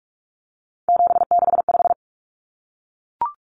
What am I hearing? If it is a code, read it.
Morse code, heard as a keyed tone, about 44 wpm, 700 Hz. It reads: 765
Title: Numbers - 3-Digit 45wpm